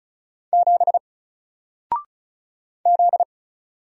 Morse code, heard as a keyed tone, about 37 wpm, 700 Hz. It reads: 7 Z
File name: Single-Letter-Number-35wpm